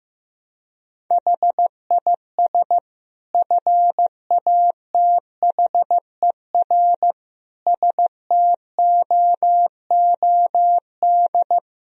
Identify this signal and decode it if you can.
Morse code, heard as a keyed tone, about 15 wpm, 700 Hz. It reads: HIS FATHER STOOD
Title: Sentences from Top 1000 Words - Review Part 2 - 15wpm